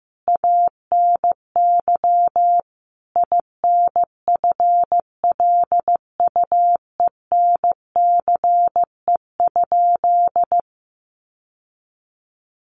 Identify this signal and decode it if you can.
Morse code, heard as a keyed tone, about 15 wpm, 700 Hz. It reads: ANY INFLUENCE?
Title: Sentences from Top 1100 Words 15wpm